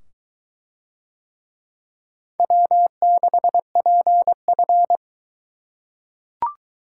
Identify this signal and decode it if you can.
Morse code, heard as a keyed tone, about 22 wpm, 700 Hz. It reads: W6PF